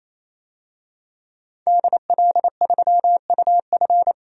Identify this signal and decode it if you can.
Morse code, heard as a keyed tone, about 27 wpm, 690 Hz. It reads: DL3UF
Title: DXCC-100 Call Signs - 5 Characters - 28wpm